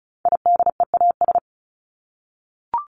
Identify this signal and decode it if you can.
Morse code, heard as a keyed tone, about 34 wpm, 700 Hz. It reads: IDEAS
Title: ES:S Suffix - Top 100 Words with Base Words 35wpm